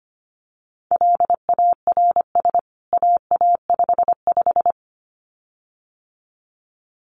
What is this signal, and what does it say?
Morse code, heard as a keyed tone, about 25 wpm, 700 Hz. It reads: LARS AA55
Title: CWT Exchange 25wpm